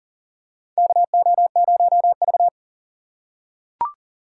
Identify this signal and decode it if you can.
Morse code, heard as a keyed tone, about 40 wpm, 690 Hz. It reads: KO0V
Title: USA Call Signs - 4 Characters - 6x Character Spacing - 40wpm